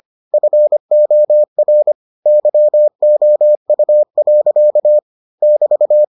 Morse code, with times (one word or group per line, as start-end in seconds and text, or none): 0.3-1.9 FOR
2.3-5.0 YOU.
5.4-6.0 <BT>